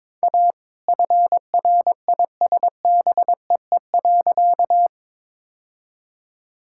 Morse code, heard as a keyed tone, about 22 wpm, 700 Hz. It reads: A FRISBEE.
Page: morse-code-ninja-practice-22wpm